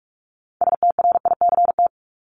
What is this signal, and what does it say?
Morse code, heard as a keyed tone, about 44 wpm, 700 Hz. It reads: STRICT
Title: Top 1501-1750 Words 45wpm